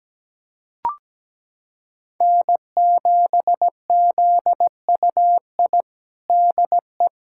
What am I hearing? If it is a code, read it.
Morse code, heard as a keyed tone, about 17 wpm, 700 Hz. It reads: N7ZUI DE
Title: USA Callsign DE Callsign - 5 Characters - 17wpm